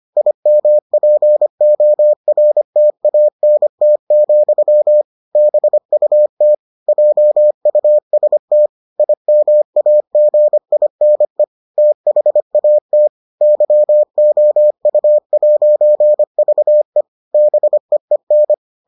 Morse code, heard as a keyed tone, about 25 wpm, 600 Hz. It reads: IMPORTANT, BUT JUST IMAGINE THAT YOU'VE BEEN